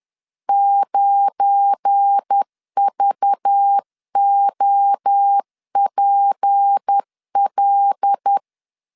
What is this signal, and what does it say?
Morse code, heard as a keyed tone, about 10 wpm, 800 Hz. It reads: 9VOPL
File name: LEZIONI / 10 SIGNS / 10D COMA 10wpm 3013